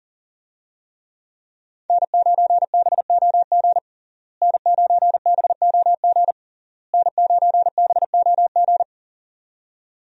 Morse code, heard as a keyed tone, about 40 wpm, 700 Hz. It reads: N9BOG N9BOG N9BOG
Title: USA Call Signs - 5 Characters - Repeated 3x - 40wpm